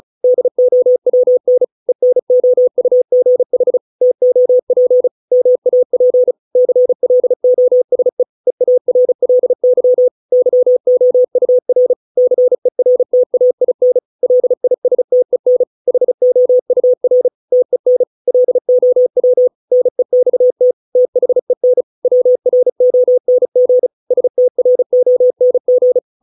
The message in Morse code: DOWN ENOUGH TOP MAP CLOSE EARLY YOUR CERTAIN LISTEN HOUR TEN LOW NEXT THEN WRONG STRONG

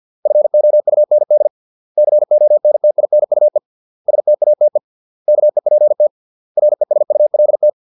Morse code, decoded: FOUND CONTINUE STATE KEPT RESULT

